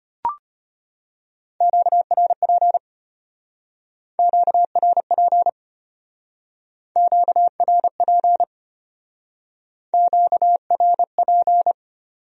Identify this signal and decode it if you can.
Morse code, heard as a keyed tone, about 31 wpm, 700 Hz. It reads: QRP QRP QRP QRP
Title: Speed-Racing Q-Codes - 25wpm